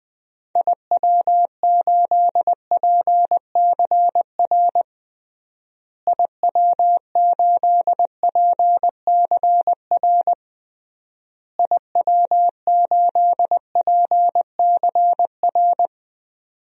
Morse code, heard as a keyed tone, about 20 wpm, 700 Hz. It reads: IW8PCR IW8PCR IW8PCR
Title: DXCC-100 Call Signs - 6 Characters - Repeated 3x - 20wpm